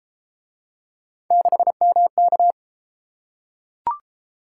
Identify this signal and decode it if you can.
Morse code, heard as a keyed tone, about 31 wpm, 700 Hz. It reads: 6MK